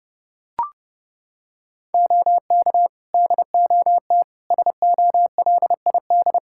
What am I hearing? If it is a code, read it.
Morse code, heard as a keyed tone, about 30 wpm, 700 Hz. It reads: OK DOT SOLID
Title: Practice QSO - Exchange 3 - More Info - 30wpm